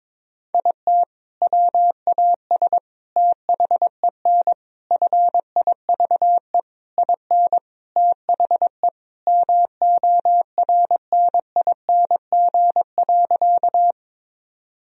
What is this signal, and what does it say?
Morse code, heard as a keyed tone, about 22 wpm, 700 Hz. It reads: IT WAS THEN FIVE IN THE MORNING.
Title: Sentences from Top 1000 Words - Review Part 2 - 22wpm